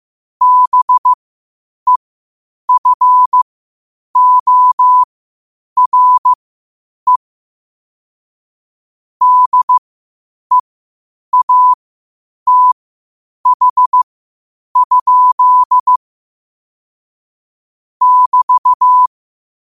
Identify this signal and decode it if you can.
Morse code, heard as a keyed tone, about 15 wpm, spaced as for 8 wpm, 1000 Hz. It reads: BEFORE DEATH? <BT>
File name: 8wpm